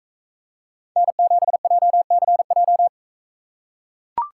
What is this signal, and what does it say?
Morse code, heard as a keyed tone, about 41 wpm, 700 Hz. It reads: N7JCJ